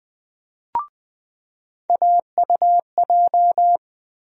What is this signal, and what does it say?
Morse code, heard as a keyed tone, about 20 wpm, 700 Hz. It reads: AUJ
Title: USA Call Sign Postfixes - 3 Letters 20wpm